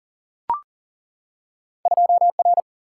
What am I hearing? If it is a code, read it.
Morse code, heard as a keyed tone, about 40 wpm, 700 Hz. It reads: 2R
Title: Single-Letter Single-Number 40wpm